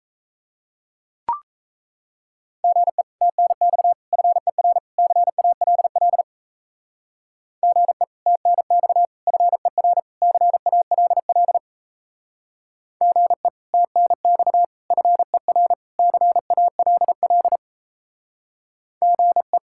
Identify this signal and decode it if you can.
Morse code, heard as a keyed tone, about 34 wpm, 700 Hz. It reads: GE TNX FER CALL GE TNX FER CALL GE TNX FER CALL GE
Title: Speed-Racing Common QSO Phrases Encore - 28wpm